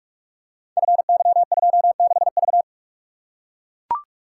Morse code, heard as a keyed tone, about 44 wpm, 700 Hz. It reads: FY26V
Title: Code Groups - 5 Letter:Numbers - 45wpm